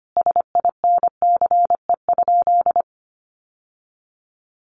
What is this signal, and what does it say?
Morse code, heard as a keyed tone, about 25 wpm, 700 Hz. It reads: SINCE?
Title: Sentences from Top 700 Words 25wpm